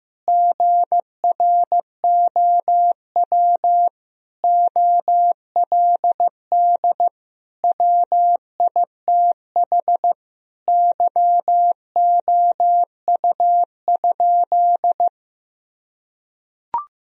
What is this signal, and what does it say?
Morse code, heard as a keyed tone, about 15 wpm, 700 Hz. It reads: GROW OLD WITH YOU?
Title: Sentences from Top 1000 Words - Review Part 1 - 15wpm